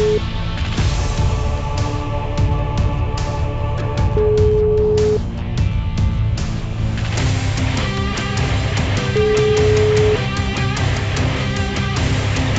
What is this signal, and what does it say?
Ringback tone (Europe, 425 Hz), with noise.